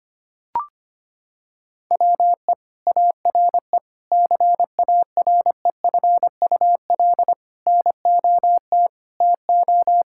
Morse code, heard as a keyed tone, about 25 wpm, 700 Hz. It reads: WE ARE CAREFUL NOT TO
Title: Sentences from Top 1600 Words 25wpm